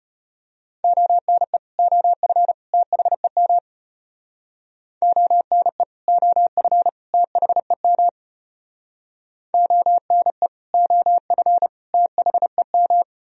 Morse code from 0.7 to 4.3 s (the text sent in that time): ONE OF THEM